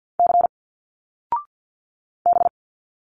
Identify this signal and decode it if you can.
Morse code, heard as a keyed tone, about 50 wpm, 700 Hz. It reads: C B
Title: Single Letters 50wpm